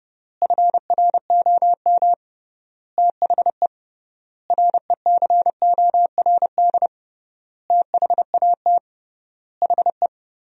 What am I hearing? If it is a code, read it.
Morse code, heard as a keyed tone, about 30 wpm, 700 Hz. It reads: FROM THE RECORD THAT HE
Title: 3x Word-Spacing - Sentences from Top 500 Words - 30wpm